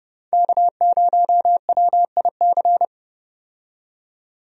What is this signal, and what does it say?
Morse code, heard as a keyed tone, about 30 wpm, 700 Hz. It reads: K0WIC
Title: USA Call Signs - 5 Characters - 30wpm